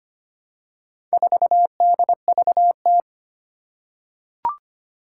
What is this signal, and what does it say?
Morse code, heard as a keyed tone, about 25 wpm, 700 Hz. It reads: 4DVT
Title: Four Challenging Letter-Number Combinations 25wpm